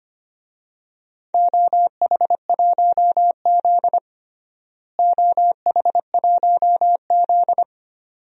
Morse code, heard as a keyed tone, about 25 wpm, 700 Hz. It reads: OH1Z OH1Z
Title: DXCC-100 Call Signs - 4 Characters - Repeated 2x - 25wpm